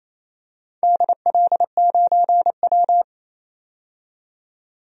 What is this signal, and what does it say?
Morse code, heard as a keyed tone, about 28 wpm, 700 Hz. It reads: DL9W
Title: DXCC-100 Call Signs - 4 Characters - 28wpm